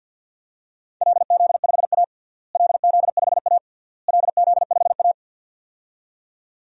Morse code, heard as a keyed tone, about 49 wpm, 700 Hz. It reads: LZ5A LZ5A LZ5A
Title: DXCC-100 Call Signs - 4 Characters - Repeated 3x - 50wpm